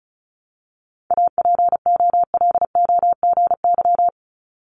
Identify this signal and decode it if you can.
Morse code, heard as a keyed tone, about 36 wpm, 700 Hz. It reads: APOLOGY
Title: Multi-letter Phonemes Repeated 3x 35wpm